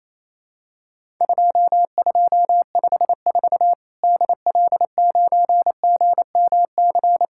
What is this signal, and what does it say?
Morse code, heard as a keyed tone, about 27 wpm, 700 Hz. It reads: 2254 DL9GMC
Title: Worked All Europe DX Contest - QTC Exchange - 28wpm